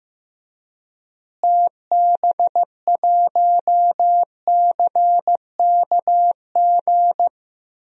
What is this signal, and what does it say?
Morse code, heard as a keyed tone, about 15 wpm, 700 Hz. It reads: TB1CKG
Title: DXCC-100 Call Signs - 6 Characters - Repeated 3x - 15wpm